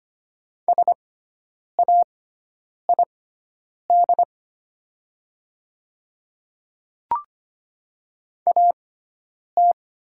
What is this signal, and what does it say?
Morse code, heard as a keyed tone, about 25 wpm, spaced as for 9 wpm, 700 Hz. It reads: SAID AT
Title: 6x Character Spacing - Top 100 Words - 25wpm